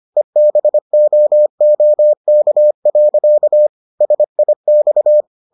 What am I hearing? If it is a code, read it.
Morse code, heard as a keyed tone, about 25 wpm, 600 Hz. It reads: EBOOK. SIX